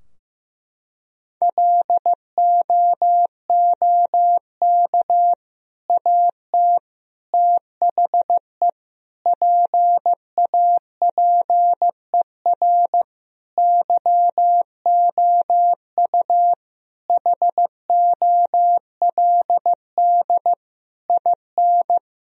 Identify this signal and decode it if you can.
Morse code, heard as a keyed tone, about 15 wpm, 700 Hz. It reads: LOOK AT THE PAPER YOU HOLD IN
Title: Sentences from Top 1000 Words - Review Part 2 - 15wpm